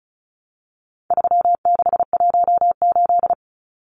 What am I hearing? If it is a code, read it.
Morse code, heard as a keyed tone, about 34 wpm, 700 Hz. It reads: 3618
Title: Numbers - 4-Digit 35wpm